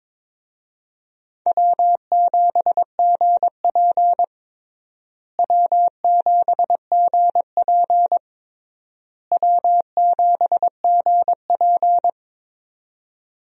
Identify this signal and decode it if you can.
Morse code, heard as a keyed tone, about 22 wpm, 700 Hz. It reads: W7GP W7GP W7GP
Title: USA Call Signs - 4 Characters - Repeated 3x - 22wpm